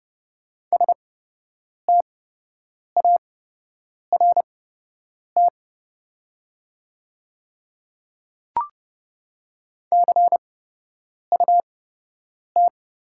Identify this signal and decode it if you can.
Morse code, heard as a keyed tone, about 30 wpm, spaced as for 8 wpm, 700 Hz. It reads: START CUT
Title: 8x Character Spacing - Top 300 Words - 30wpm